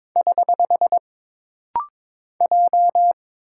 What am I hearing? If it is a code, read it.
Morse code, heard as a keyed tone, about 22 wpm, 700 Hz. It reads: <HH> J
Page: This rapid-fire timing is geared to push your ICR (Instant Character Recognition) skills.